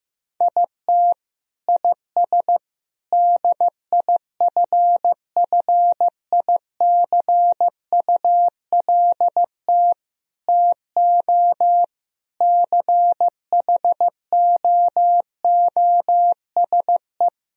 Morse code, heard as a keyed tone, about 15 wpm, 700 Hz. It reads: IT IS DIFFICULT TO CHOOSE
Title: Sentences from Top 1000 Words - Review Part 3 - 15wpm